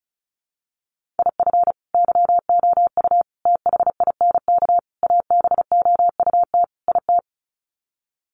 Morse code, heard as a keyed tone, about 34 wpm, 700 Hz. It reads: IF YOU THINK ABOUT IT
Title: Sets of 5 Words Encore2 35wpm